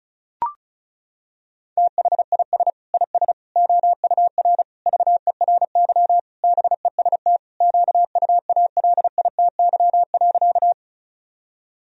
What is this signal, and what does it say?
Morse code, heard as a keyed tone, about 34 wpm, 700 Hz. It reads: THIS IS OUR VERY BEST QUALITY.
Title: Sentences from Top 1000 Words - Review Part 4 - 35wpm